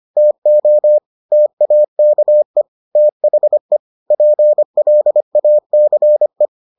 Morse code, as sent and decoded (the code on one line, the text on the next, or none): - --- / - .- -.- . / - .... / .--. .-.. .- -.-. .
TO TAKE THE PLACE